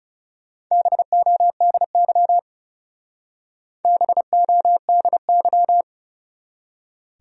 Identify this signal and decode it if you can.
Morse code, heard as a keyed tone, about 36 wpm, 700 Hz. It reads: BODY BODY
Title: Speed-Racing Top 500 Words 30wpm